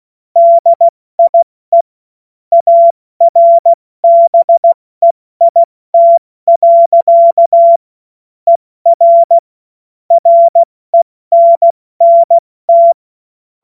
Morse code, heard as a keyed tone, about 16 wpm, spaced as for 14 wpm, 680 Hz. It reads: DIE ARBEIT. ER RENNT
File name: Morsekurs / Witze / Witz 21. Dezember 2021, 70 F80